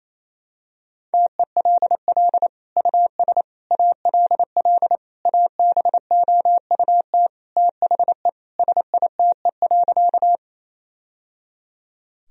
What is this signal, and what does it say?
Morse code, heard as a keyed tone, about 28 wpm, 700 Hz. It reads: TELL US ALL ABOUT THE SITE.
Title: Sentences from Top 1000 Words - Review Part 4 - 28wpm